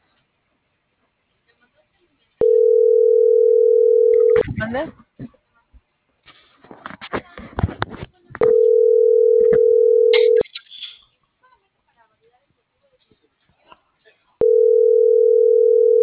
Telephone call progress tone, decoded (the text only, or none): ringback tone